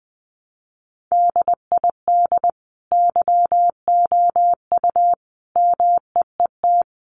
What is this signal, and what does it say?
Morse code, heard as a keyed tone, about 20 wpm, 700 Hz. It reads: DID YOU MEET
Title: Sentences from Top 1100 Words 20wpm